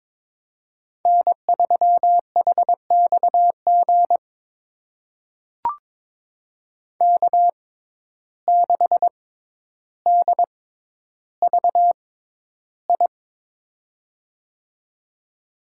Morse code, heard as a keyed tone, about 22 wpm, spaced as for 11 wpm, 700 Hz. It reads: N3HXG K6DVI
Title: USA Call Signs - 6x Character Spacing - 22wpm